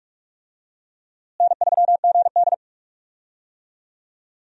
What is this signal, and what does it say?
Morse code, heard as a keyed tone, about 43 wpm, 700 Hz. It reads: N3GD
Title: USA Call Signs - 4 Characters - Repeated 3x - 45wpm